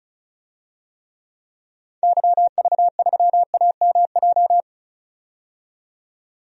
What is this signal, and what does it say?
Morse code, heard as a keyed tone, about 36 wpm, 700 Hz. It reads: YV3AMJ
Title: DXCC-100 Call Signs - 6 Characters - 35wpm